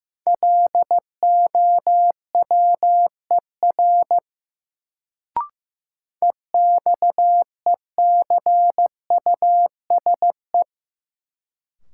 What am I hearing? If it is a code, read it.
Morse code, heard as a keyed tone, about 15 wpm, 700 Hz. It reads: LOWER EXECUSE
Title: Sentences from Top 700 Words - New Word Drill - 15wpm